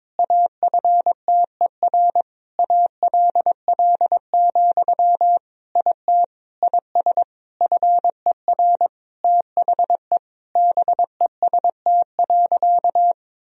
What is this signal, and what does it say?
Morse code, heard as a keyed tone, about 22 wpm, 700 Hz. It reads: AFTER ALL, IT IS FER THE BEST.
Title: Sentences from Top 500 Words and Abbreviations 22wpm